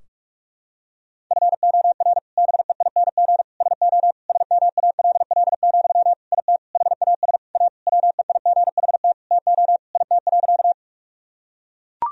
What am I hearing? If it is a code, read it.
Morse code, heard as a keyed tone, about 43 wpm, 700 Hz. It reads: FOR BEING SO SMALL, IT HAS A WEIGHT TO IT.
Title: Sentences from Top 1000 Words - Review Part 3 - 45wpm